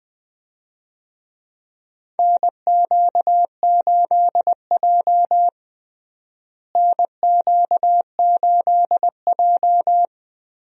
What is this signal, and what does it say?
Morse code, heard as a keyed tone, about 20 wpm, 700 Hz. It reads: NQ8J NQ8J
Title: USA Call Signs - 4 Characters - Repeated 2x - 20wpm